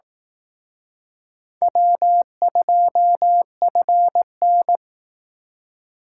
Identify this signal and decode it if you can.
Morse code, heard as a keyed tone, about 19 wpm, 700 Hz. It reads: W2FN